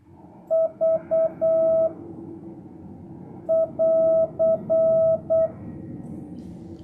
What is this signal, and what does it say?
Morse code, heard as a keyed tone, about 8 wpm, 660 Hz. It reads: V <AR>